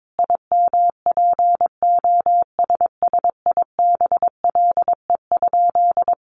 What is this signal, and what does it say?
Morse code, heard as a keyed tone, about 22 wpm, 700 Hz. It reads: IMPOSSIBLE?
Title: Sentences from Top 1100 Words 22wpm